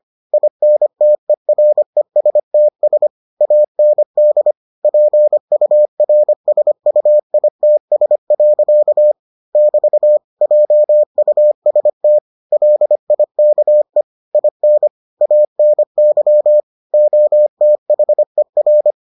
Morse code, heard as a keyed tone, about 25 wpm, 600 Hz. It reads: INTERESTS AND PURSUITS. <BT> JUST LIKE IN ANY OTHER